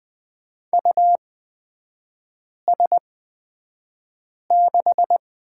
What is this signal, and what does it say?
Morse code, heard as a keyed tone, about 20 wpm, spaced as for 5 wpm, 700 Hz. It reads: US6
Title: 224 Lesson 36e - BT - Three Characters 20wpm